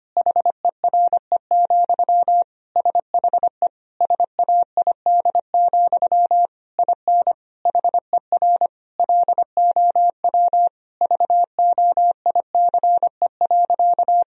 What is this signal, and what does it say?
Morse code, heard as a keyed tone, about 25 wpm, 700 Hz. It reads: HERE, SHE SAID, IN HER LOW VOICE.
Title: Sentences from Top 1000 Words - Review Part 1 - 25wpm